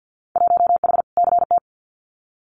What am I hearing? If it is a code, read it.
Morse code, heard as a keyed tone, about 50 wpm, 700 Hz. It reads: 1H CT
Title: Winter Field Day Exchanges Encore 50wpm